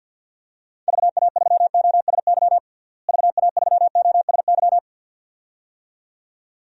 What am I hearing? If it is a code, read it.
Morse code, heard as a keyed tone, about 52 wpm, 700 Hz. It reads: VA3OSY VA3OSY
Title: DXCC-100 Call Signs - 6 Characters - Repeated 2x - 50wpm